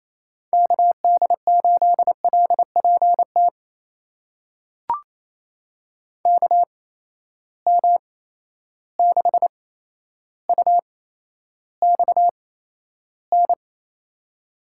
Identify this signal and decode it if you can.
Morse code, heard as a keyed tone, about 28 wpm, spaced as for 12 wpm, 700 Hz. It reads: KD8LPT KM6UXN
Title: USA Call Signs - 6 Characters - 8x Character Spacing - 28wpm